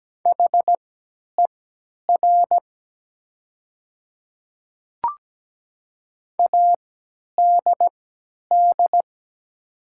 Morse code, heard as a keyed tone, about 17 wpm, spaced as for 10 wpm, 700 Hz. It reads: HER ADD
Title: 3x Character Spacing - Top 200 Words - 17wpm